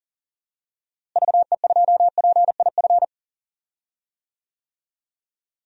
Morse code, heard as a keyed tone, about 40 wpm, 700 Hz. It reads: VE2PIF
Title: DXCC-100 Call Signs - 6 Characters - Repeated 2x - 40wpm